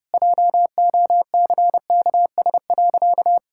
Morse code, heard as a keyed tone, about 32 wpm, 700 Hz. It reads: JOCKS.
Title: Pangram Challenge v2 - 30wpm